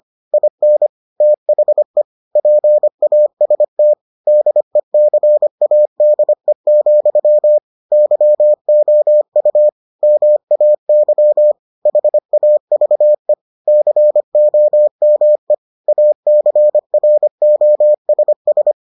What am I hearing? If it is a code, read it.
Morse code, heard as a keyed tone, about 25 wpm, 600 Hz. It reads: IN THE PAST DECADE, YOU MAY HAVE COME ACROSS